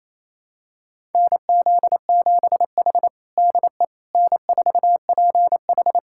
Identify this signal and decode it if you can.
Morse code, heard as a keyed tone, about 28 wpm, 700 Hz. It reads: NZ7H DE N4PH
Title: USA Callsign DE Callsign - 4 Characters - 28wpm